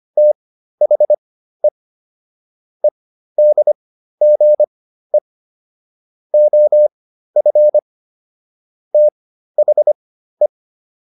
Morse code, heard as a keyed tone, about 25 wpm, spaced as for 13 wpm, 600 Hz. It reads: THE EDGE OF THE